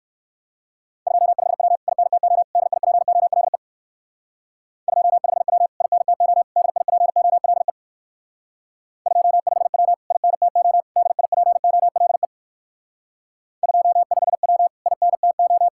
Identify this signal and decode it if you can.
Morse code, heard as a keyed tone, about 59 wpm, 700 Hz. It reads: 25W INTO DIPOLE 25W INTO DIPOLE 25W INTO DIPOLE 25W INTO